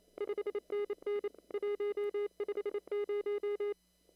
Morse code, heard as a keyed tone, about 27 wpm, 410 Hz. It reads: 5NN 150
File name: CW / kt5j wpx cw-op-k5pi